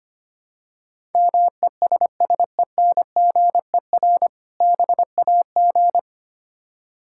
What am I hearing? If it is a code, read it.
Morse code, heard as a keyed tone, about 25 wpm, 700 Hz. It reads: MESSENGER BAG
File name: practice / 25wpm / Difficult Words - Speed-Racing with Hints - 25wpm